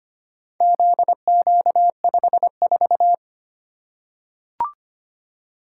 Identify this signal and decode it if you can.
Morse code, heard as a keyed tone, about 25 wpm, 700 Hz. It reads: ZQ54
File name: practice / 25wpm / Four Letter-Number Combinations - 25wpm